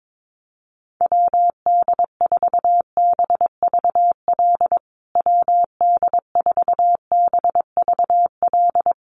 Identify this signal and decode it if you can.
Morse code, heard as a keyed tone, about 22 wpm, 700 Hz. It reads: WD4BVL WD4BVL
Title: USA Call Signs Repeated 2x 22wpm